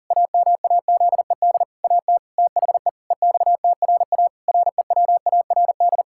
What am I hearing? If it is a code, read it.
Morse code, heard as a keyed tone, about 40 wpm, 700 Hz. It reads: AMAZED AT THE EXTRA REWARD
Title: Pangram Challenge v2 - 40wpm